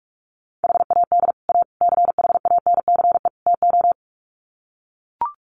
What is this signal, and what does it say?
Morse code, heard as a keyed tone, about 44 wpm, 700 Hz. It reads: HAD A CHANCE TO
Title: Sets of 4 Words 45wpm